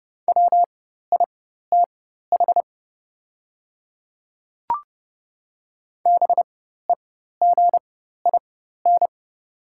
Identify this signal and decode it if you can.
Morse code, heard as a keyed tone, about 30 wpm, spaced as for 14 wpm, 700 Hz. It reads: WITH BEGIN
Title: 4x Character Spacing - Top 300 Words - 30wpm